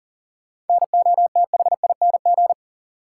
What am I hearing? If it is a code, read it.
Morse code, heard as a keyed tone, about 40 wpm, 700 Hz. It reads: NOTHING
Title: Sets of 4 Words - New Word Drill - 40wpm